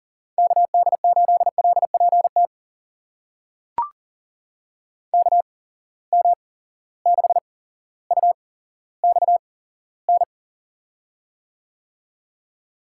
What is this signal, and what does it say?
Morse code, heard as a keyed tone, about 39 wpm, spaced as for 17 wpm, 700 Hz. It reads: KD8LPT KM6UXN
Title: USA Call Signs - 6 Characters - 8x Character Spacing - 40wpm